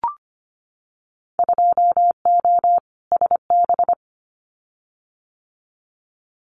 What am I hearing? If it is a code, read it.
Morse code, heard as a keyed tone, about 25 wpm, 700 Hz. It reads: 2O SB
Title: Winter Field Day Exchanges 25wpm